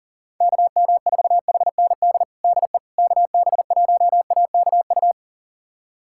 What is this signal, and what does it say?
Morse code, heard as a keyed tone, about 37 wpm, 700 Hz. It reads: KM4HND DE KB1AKU
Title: USA Callsign DE Callsign 40wpm